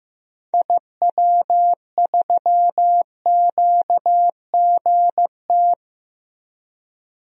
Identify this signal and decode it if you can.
Morse code, heard as a keyed tone, about 15 wpm, 690 Hz. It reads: IW3QGT
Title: DXCC-100 Call Signs - 6 Characters - 15wpm